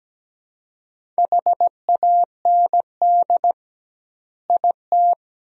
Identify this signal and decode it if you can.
Morse code, heard as a keyed tone, about 17 wpm, 700 Hz. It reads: HAND IT
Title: 2x Word-Spacing - Sentences from Top 200 Words - 17wpm